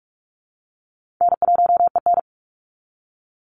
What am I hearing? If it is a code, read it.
Morse code, heard as a keyed tone, about 44 wpm, 700 Hz. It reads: N1EN
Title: Easy Call Signs Repeated 3x - 45wpm